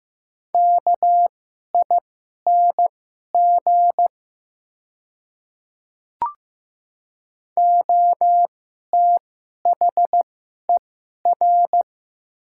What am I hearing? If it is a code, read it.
Morse code, heard as a keyed tone, about 15 wpm, spaced as for 11 wpm, 700 Hz. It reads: KING OTHER
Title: 2x Character Spacing - Top 500 Words - 15wpm